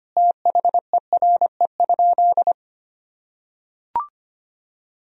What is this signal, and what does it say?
Morse code, heard as a keyed tone, about 25 wpm, 700 Hz. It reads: THERE?
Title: Sentences from Top 1000 Words - Review Part 4 - 25wpm